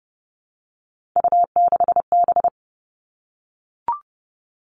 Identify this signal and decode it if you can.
Morse code, heard as a keyed tone, about 30 wpm, 700 Hz. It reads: U6B
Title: All Cut Numbers - 3-Digit 30wpm